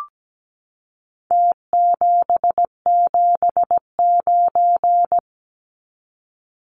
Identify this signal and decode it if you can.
Morse code, heard as a keyed tone, about 17 wpm, 700 Hz. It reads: T779
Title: Numbers - Contest Style - 4-Digit 17wpm